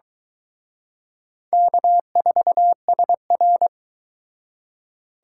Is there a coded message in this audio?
Yes: Morse code from 1.5 to 3.7 s, heard as a keyed tone, about 23 wpm, 700 Hz. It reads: K4SR